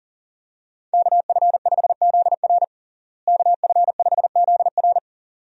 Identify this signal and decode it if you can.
Morse code, heard as a keyed tone, about 40 wpm, 700 Hz. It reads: KF5ZR KF5ZR
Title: USA Call Signs - 5 Characters - Repeated 2x - 40wpm